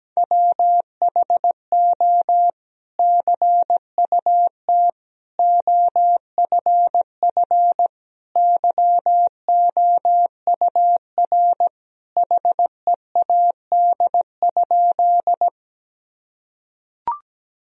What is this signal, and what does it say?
Morse code, heard as a keyed tone, about 17 wpm, 700 Hz. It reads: WHO CUT OFF YOUR HEAD?
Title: Sentences from Top 1000 Words - Review Part 1 - 17wpm